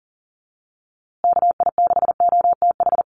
Morse code, heard as a keyed tone, about 40 wpm, 700 Hz. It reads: KI6OTH
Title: USA Call Signs Repeated 2x 40wpm